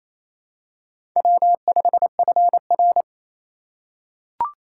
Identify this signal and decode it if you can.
Morse code, heard as a keyed tone, about 29 wpm, 700 Hz. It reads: W5FR
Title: USA Call Signs - 4 Characters - 4x Character Spacing - 28wpm